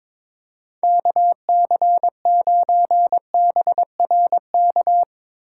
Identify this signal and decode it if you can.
Morse code, heard as a keyed tone, about 22 wpm, 700 Hz. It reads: KC9BRK
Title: USA Call Signs - 6x Character Spacing - 22wpm